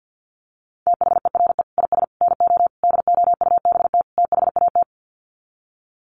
Morse code, heard as a keyed tone, about 50 wpm, 700 Hz. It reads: THERE IS NO DOUBT THAT
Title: Sets of 5 Words 50wpm